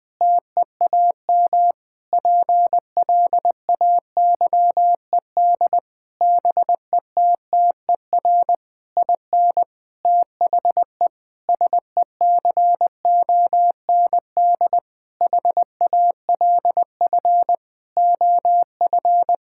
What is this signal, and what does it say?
Morse code, heard as a keyed tone, about 20 wpm, 700 Hz. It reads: TEAM PLAYED BETTER IN THE SECOND HALF OF
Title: Sentences with 6-Word Phrases - 20wpm